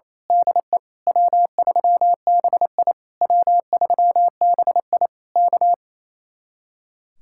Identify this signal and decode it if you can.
Morse code, heard as a keyed tone, about 26 wpm, 700 Hz. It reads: DE W3BI W3BI K
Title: Speed-Racing Common QSO Phrases Encore - 28wpm